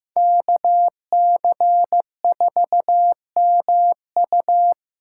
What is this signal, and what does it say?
Morse code, heard as a keyed tone, about 16 wpm, 700 Hz. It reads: KC4MU
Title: USA Callsign DE Callsign - 5 Characters - 15wpm